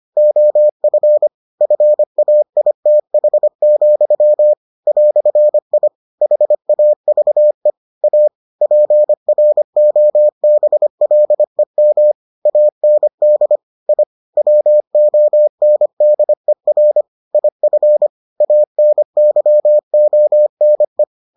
Morse code, heard as a keyed tone, about 25 wpm, 600 Hz. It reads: OF FAITH, "I HAVE A PROBLEM AND I WONDER IF ANYONE